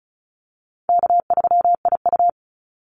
Morse code, heard as a keyed tone, about 34 wpm, 700 Hz. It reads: K3IU
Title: Call Signs - 35wpm